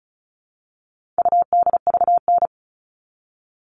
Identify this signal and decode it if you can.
Morse code, heard as a keyed tone, about 34 wpm, 700 Hz. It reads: UDVN
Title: All Cut Numbers - 4-Digit 35wpm